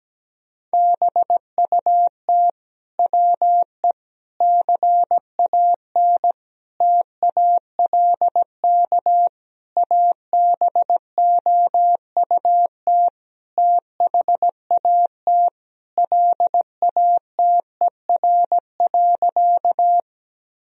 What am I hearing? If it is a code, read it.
Morse code, heard as a keyed tone, about 17 wpm, 700 Hz. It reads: BUT WE CAN TALK ABOUT THAT LATER.
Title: Sentences from Top 1000 Words - Review Part 4 - 17wpm